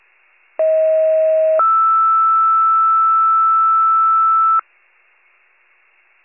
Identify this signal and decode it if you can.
Two-tone page, heard as a one-second tone, then a three-second tone, about 640 then 1320 Hz. It is heard from 0.6 to 4.6 s.
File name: pager / 38 FIRE / wcalert